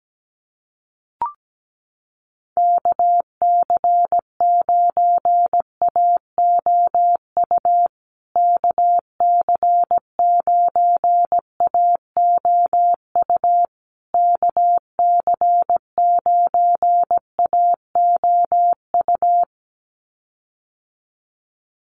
Morse code, heard as a keyed tone, about 17 wpm, 700 Hz. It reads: KC9AOU KC9AOU KC9AOU
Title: USA Call Signs Repeated 3x 17wpm